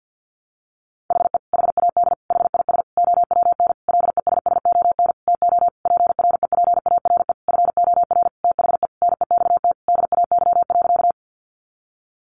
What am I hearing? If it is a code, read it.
Morse code, heard as a keyed tone, about 50 wpm, 700 Hz. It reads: HE HAD HIS OWN LESSON TO PREPARE FOR THE NEXT DAY.
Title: Sentences from Top 1200 Words 50wpm